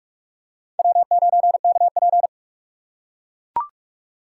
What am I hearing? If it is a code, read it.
Morse code, heard as a keyed tone, about 41 wpm, 700 Hz. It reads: W9KP
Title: USA Call Signs - 2x Character Spacing - 45wpm